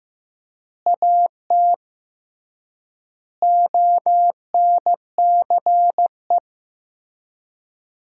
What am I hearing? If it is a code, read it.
Morse code, heard as a keyed tone, about 15 wpm, 700 Hz. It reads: AT ONCE
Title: 3x Word-Spacing - Sentences from Top 300 Words - 15wpm